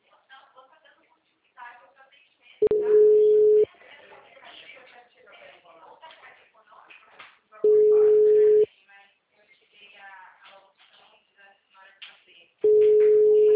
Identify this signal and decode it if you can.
Ringback tone (Europe, 425 Hz), with speech.